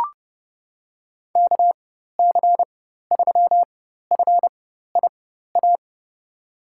Morse code, heard as a keyed tone, about 30 wpm, spaced as for 14 wpm, 700 Hz. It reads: KC3FIA
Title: USA Call Signs - 4x Character Spacing - 30wpm